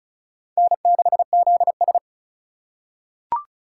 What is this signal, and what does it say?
Morse code, heard as a keyed tone, about 34 wpm, 700 Hz. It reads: N6ZS
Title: USA Call Signs - 4 Characters - 4x Character Spacing - 35wpm